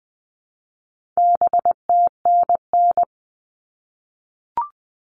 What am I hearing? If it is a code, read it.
Morse code, heard as a keyed tone, about 20 wpm, 700 Hz. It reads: BTNN
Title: All Cut Numbers - 4-Digit 20wpm